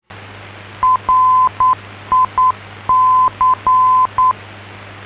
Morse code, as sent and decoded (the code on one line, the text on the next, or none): .-. .. -.-.
RIC